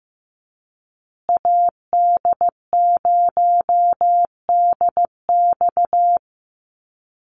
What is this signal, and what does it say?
Morse code, heard as a keyed tone, about 14 wpm, 700 Hz. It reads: AD0DX
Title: Call Signs Repeated 2x - 15wpm